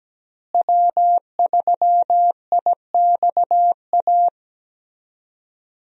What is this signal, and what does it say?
Morse code, heard as a keyed tone, about 17 wpm, 700 Hz. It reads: W3IXA
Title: USA Call Signs - 5 Characters - Repeated 3x - 17wpm